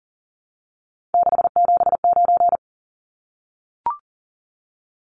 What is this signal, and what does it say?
Morse code, heard as a keyed tone, about 40 wpm, 700 Hz. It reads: 679
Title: Numbers - 3-Digit 40wpm